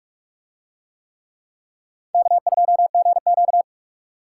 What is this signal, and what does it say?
Morse code, heard as a keyed tone, about 45 wpm, 700 Hz. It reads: K2GQ